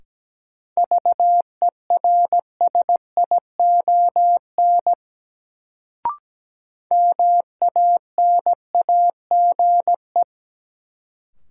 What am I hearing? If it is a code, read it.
Morse code, heard as a keyed tone, about 17 wpm, 700 Hz. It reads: VERSION MANAGE
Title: Sentences from Top 1400 Words - New Word Drill Alt - 17wpm